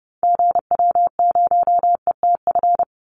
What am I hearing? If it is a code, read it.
Morse code, heard as a keyed tone, about 30 wpm, 700 Hz. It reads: GW0ETF
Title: Call Signs - 30wpm